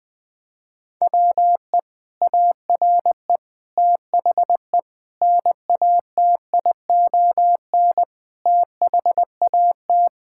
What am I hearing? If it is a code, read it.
Morse code, heard as a keyed tone, about 21 wpm, 700 Hz. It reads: WE ARE THE NATION THAT
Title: Sentences from Top 1000 Words - Review Part 6 - 20wpm